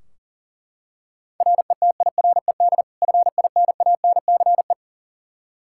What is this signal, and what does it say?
Morse code, heard as a keyed tone, about 40 wpm, 700 Hz. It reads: RETIRED FINANCE
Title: Practice QSO - Exchange 3 - More Info - 40wpm